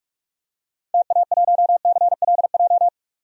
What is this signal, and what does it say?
Morse code, heard as a keyed tone, about 42 wpm, 700 Hz. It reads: TA1CLJ
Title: DXCC-100 Call Signs - 6 Characters - 45wpm